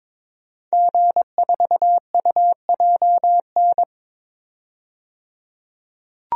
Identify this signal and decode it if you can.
Morse code, heard as a keyed tone, about 22 wpm, 700 Hz. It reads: G4UJN
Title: DXCC-100 Call Signs - 5 Characters - 22wpm